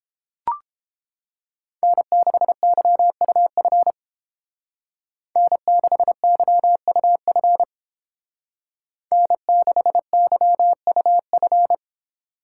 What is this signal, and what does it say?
Morse code, heard as a keyed tone, about 31 wpm, 700 Hz. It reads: N6YUF N6YUF N6YUF